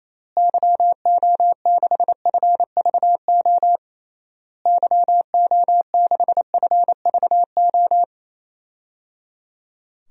Morse code, heard as a keyed tone, about 28 wpm, 700 Hz. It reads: YO6FVO YO6FVO
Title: DXCC-100 Call Signs - 6 Characters - Repeated 3x - 28wpm